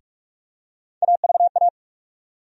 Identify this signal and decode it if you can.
Morse code, heard as a keyed tone, about 43 wpm, 700 Hz. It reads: AVA
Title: USA Call Sign Postfixes - 3 Letters 45wpm